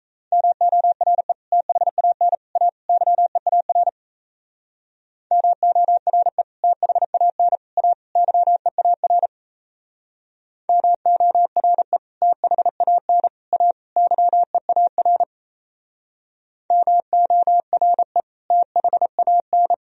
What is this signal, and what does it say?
Morse code, heard as a keyed tone, about 36 wpm, 700 Hz. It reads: MORE THAN A YEAR MORE THAN A YEAR MORE THAN A YEAR MORE THAN